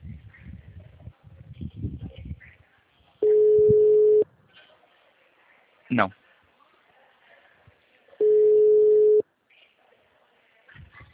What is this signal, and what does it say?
Ringback tone (Europe, 425 Hz), with speech.